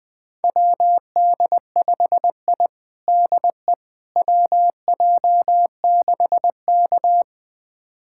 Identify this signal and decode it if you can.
Morse code, heard as a keyed tone, about 20 wpm, 700 Hz. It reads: WD5I DE WJ6K
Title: USA Callsign DE Callsign - 4 Characters - 20wpm